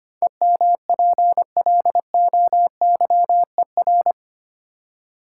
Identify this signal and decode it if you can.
Morse code, heard as a keyed tone, about 25 wpm, 700 Hz. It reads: EMPLOYER
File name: Eight Letter Words - 25wpm